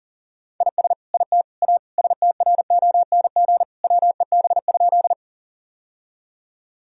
Morse code, heard as a keyed tone, about 40 wpm, 700 Hz. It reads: IS IT A STRONG WEB?
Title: Sentences from Top 1000 Words - Review Part 4 - 40wpm